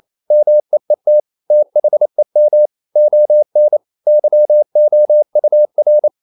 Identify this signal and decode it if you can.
Morse code, heard as a keyed tone, about 28 wpm, 600 Hz. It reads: MEET THEM ON YOUR
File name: CW / Practice / EnglishQuotations0228